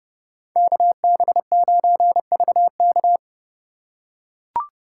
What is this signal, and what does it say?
Morse code, heard as a keyed tone, about 31 wpm, 700 Hz. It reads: KB9VK
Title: USA Call Signs - 4x Character Spacing - 30wpm